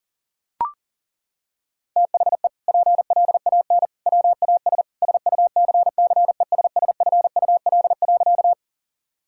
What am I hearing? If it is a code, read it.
Morse code, heard as a keyed tone, about 40 wpm, 700 Hz. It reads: THE PLAN WAS SUCCESSFUL.
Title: Sentences from Top 1000 Words - Review Part 6 - 40wpm